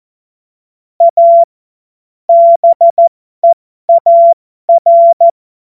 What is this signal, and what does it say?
Morse code, heard as a keyed tone, about 14 wpm, spaced as for 12 wpm, 680 Hz. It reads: A BEAR
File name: Morsekurs / Witze